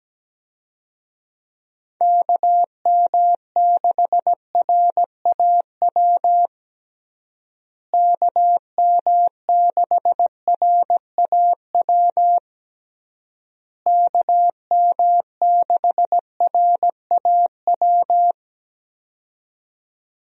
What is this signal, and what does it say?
Morse code, heard as a keyed tone, about 17 wpm, 700 Hz. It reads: KM6RAW KM6RAW KM6RAW
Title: USA Call Signs - 6 Characters - Repeated 3x - 17wpm